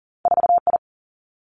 Morse code, heard as a keyed tone, about 38 wpm, 700 Hz. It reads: 4 I
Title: Two Challenging Letter-Number Combinations 40wpm